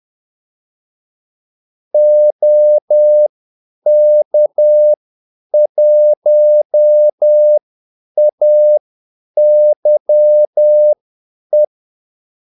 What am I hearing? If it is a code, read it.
Morse code, heard as a keyed tone, about 10 wpm, spaced as for 8 wpm, 600 Hz. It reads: OK1AYE
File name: Callsigns-10-8wpm